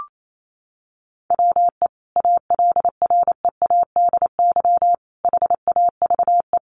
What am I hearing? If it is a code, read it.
Morse code, heard as a keyed tone, about 28 wpm, 700 Hz. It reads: WE ALREADY HAVE
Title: Sentences from Top 700 Words 28wpm